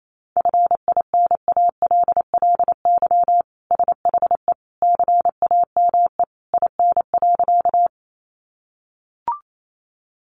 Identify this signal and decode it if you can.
Morse code, heard as a keyed tone, about 28 wpm, 700 Hz. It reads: FINALLY SHE CAME IN.
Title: Sentences from Top 1000 Words 28wpm